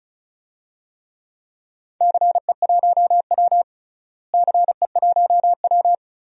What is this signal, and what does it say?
Morse code, heard as a keyed tone, about 36 wpm, 700 Hz. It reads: CE1W CE1W
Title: DXCC-100 Call Signs - 4 Characters - Repeated 2x - 35wpm